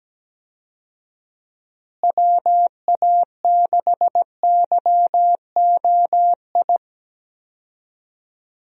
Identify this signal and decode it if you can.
Morse code, heard as a keyed tone, about 17 wpm, 700 Hz. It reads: WA6YOI
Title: USA Call Signs - 6 Characters - Repeated 3x - 17wpm